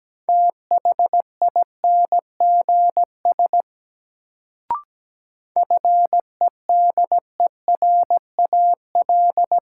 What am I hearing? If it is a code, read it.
Morse code, heard as a keyed tone, about 17 wpm, 700 Hz. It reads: THINGS FEDERAL
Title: Sets of 3 Words - New Word Drill - 17wpm